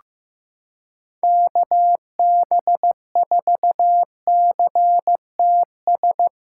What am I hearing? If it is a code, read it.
Morse code, heard as a keyed tone, about 15 wpm, 700 Hz. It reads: KB4CTS
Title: USA Callsign DE Callsign 15wpm